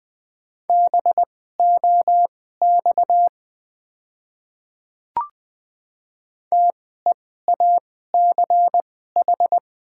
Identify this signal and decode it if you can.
Morse code, heard as a keyed tone, about 20 wpm, spaced as for 14 wpm, 700 Hz. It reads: BOX TEACH
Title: 2x Character Spacing - Top 500 Words - 20wpm